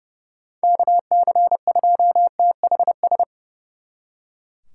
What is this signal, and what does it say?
Morse code, heard as a keyed tone, about 38 wpm, 700 Hz. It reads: KC2THS